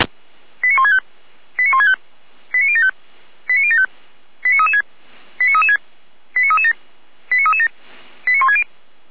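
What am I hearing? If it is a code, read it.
Five-tone selective call (ZVEI), heard as quick runs of five tones, tones of about 80 ms. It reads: 89165 89165 89075 89075 89336 89336 89337 89337 89160
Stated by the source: ZVEI five-tone selective call received on SDR.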